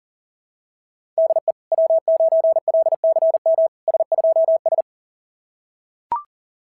Morse code, heard as a keyed tone, about 37 wpm, 660 Hz. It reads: DE W9LCM S2S
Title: SOTA QSO Practice 40wpm